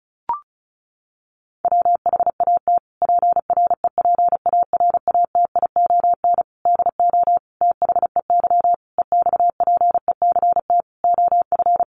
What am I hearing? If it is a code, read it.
Morse code, heard as a keyed tone, about 34 wpm, 700 Hz. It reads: WHAT PREPARATION DO THEY EXPECT OF
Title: Sentences from Top 1400 Words 35wpm